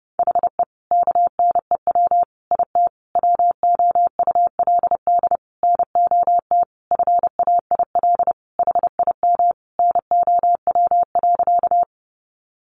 Morse code, heard as a keyed tone, about 30 wpm, 700 Hz. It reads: HE KNEW IT WOULD NOT FAIL HIM NOW.
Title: Sentences from Top 1100 Words 30wpm